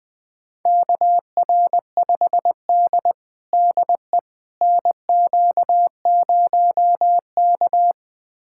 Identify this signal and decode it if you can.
Morse code, heard as a keyed tone, about 20 wpm, 700 Hz. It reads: KR5D DE NQ0K
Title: USA Callsign DE Callsign - 4 Characters - 20wpm